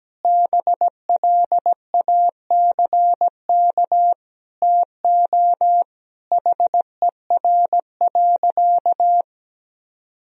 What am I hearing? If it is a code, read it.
Morse code, heard as a keyed tone, about 17 wpm, 700 Hz. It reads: BLACK TO HER.
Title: Sentences from Top 1000 Words - Review Part 2 - 17wpm